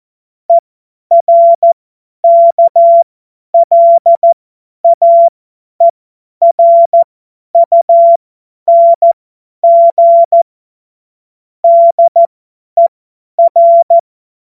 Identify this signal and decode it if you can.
Morse code, heard as a keyed tone, about 14 wpm, spaced as for 10 wpm, 680 Hz. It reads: ERKLAERUNG DER
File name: Morsekurs / Witze / Witz 12. April 2022 50 F70